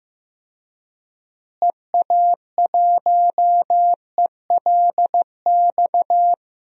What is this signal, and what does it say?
Morse code, heard as a keyed tone, about 15 wpm, 700 Hz. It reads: EA1ELX
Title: DXCC-100 Call Signs - 6 Characters - Repeated 2x - 15wpm